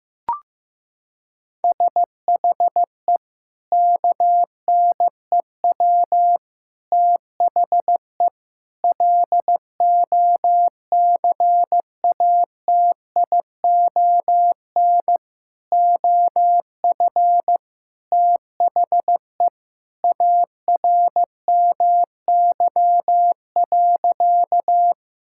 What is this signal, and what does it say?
Morse code, heard as a keyed tone, about 15 wpm, 700 Hz. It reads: SHE KNEW THE LOCATION OF THE ARMY.
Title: Sentences from Top 1000 Words - Review Part 6 - 15wpm